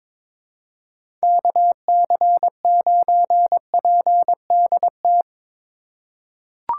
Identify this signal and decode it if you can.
Morse code, heard as a keyed tone, about 22 wpm, 700 Hz. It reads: KC9PDT
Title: USA Call Signs - 6 Characters - 2x Character Spacing - 22wpm